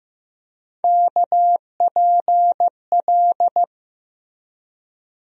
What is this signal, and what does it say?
Morse code, heard as a keyed tone, about 15 wpm, 700 Hz. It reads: KPL
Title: USA Call Sign Postfixes - 3 Letters 15wpm